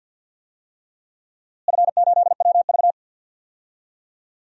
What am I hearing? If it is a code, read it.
Morse code, heard as a keyed tone, about 52 wpm, 700 Hz. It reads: F8WV
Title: DXCC-100 Call Signs - 4 Characters - Repeated 3x - 50wpm